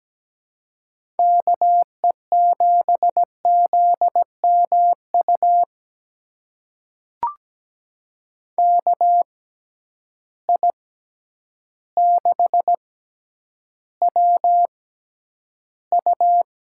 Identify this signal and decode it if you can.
Morse code, heard as a keyed tone, about 17 wpm, spaced as for 9 wpm, 700 Hz. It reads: KE7ZMU KI6WU
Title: USA Call Signs - 6 Characters - 6x Character Spacing - 17wpm